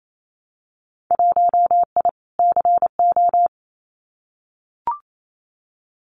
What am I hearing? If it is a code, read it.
Morse code, heard as a keyed tone, about 28 wpm, 700 Hz. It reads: 1I CO
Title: Winter Field Day Exchanges Encore 28wpm